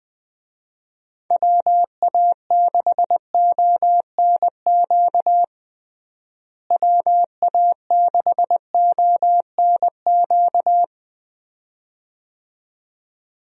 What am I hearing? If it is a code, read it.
Morse code, heard as a keyed tone, about 20 wpm, 700 Hz. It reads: WA6ONQ WA6ONQ
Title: USA Call Signs - 6 Characters - Repeated 2x - 20wpm